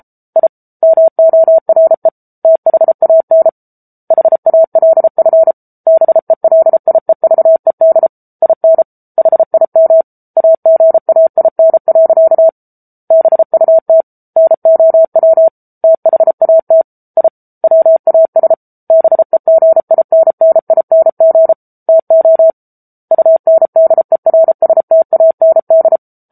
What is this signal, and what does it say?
Morse code, heard as a keyed tone, about 34 wpm, 650 Hz. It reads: I MORE THAN HALF BELIEVED IN HIM AGAIN. BUT NOW THAT I WAS BEGINNING TO UNDERSTAND
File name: Book Two / 07-Chapter 7 - 33 WPM